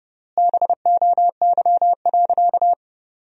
Morse code, heard as a keyed tone, about 30 wpm, 700 Hz. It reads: BOY.
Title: 3x Word-Spacing - Sentences from Top 200 Words - 30wpm